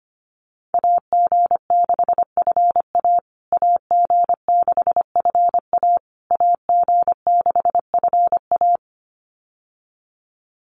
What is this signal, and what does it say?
Morse code, heard as a keyed tone, about 25 wpm, 700 Hz. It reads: AG6FA AG6FA AG6FA
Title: USA Call Signs Repeated 3x 25wpm